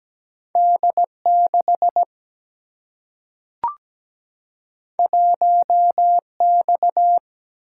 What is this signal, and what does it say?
Morse code, heard as a keyed tone, about 17 wpm, 700 Hz. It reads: D6 1X
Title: Single-Letter Single-Number 17wpm